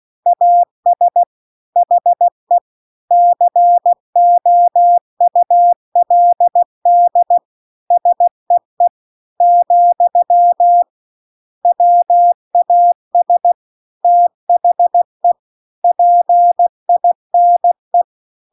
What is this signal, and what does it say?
Morse code, heard as a keyed tone, about 16 wpm, 700 Hz. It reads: AS HE COULD SEE , WAS THE PINE